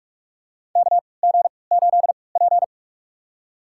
Morse code, heard as a keyed tone, about 45 wpm, spaced as for 25 wpm, 700 Hz. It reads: KG8P
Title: USA Call Signs - 3x Character Spacing - 45wpm